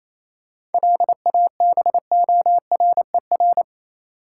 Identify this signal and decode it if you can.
Morse code, heard as a keyed tone, about 28 wpm, 700 Hz. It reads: LABORER
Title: Practice QSO - Exchange 3 - More Info - 28wpm